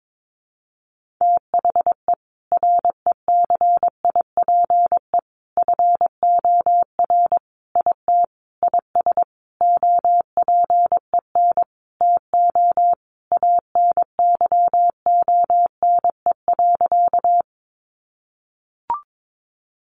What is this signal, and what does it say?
Morse code, heard as a keyed tone, about 22 wpm, 700 Hz. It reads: THE RECIPE FOR IT IS OPEN TO ANYONE.
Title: Sentences from Top 1200 Words 22wpm